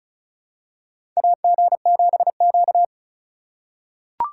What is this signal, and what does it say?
Morse code, heard as a keyed tone, about 36 wpm, 700 Hz. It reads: AG7Q
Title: USA Call Signs - 4 Characters - 2x Character Spacing - 35wpm